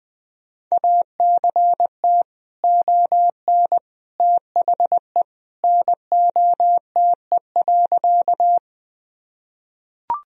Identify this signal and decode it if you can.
Morse code, heard as a keyed tone, about 20 wpm, 700 Hz. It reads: ACT ON THE NOTE.
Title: Sentences from Top 1000 Words - Review Part 2 - 20wpm